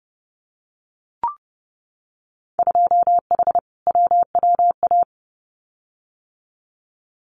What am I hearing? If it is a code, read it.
Morse code, heard as a keyed tone, about 30 wpm, 700 Hz. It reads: 2H WWA
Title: Winter Field Day Exchanges 30wpm